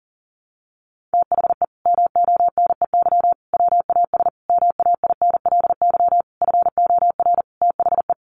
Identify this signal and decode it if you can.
Morse code, heard as a keyed tone, about 40 wpm, 700 Hz. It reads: THE MONEY WAS MAINLY FOR THE
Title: Sentences from Top 1400 Words 40wpm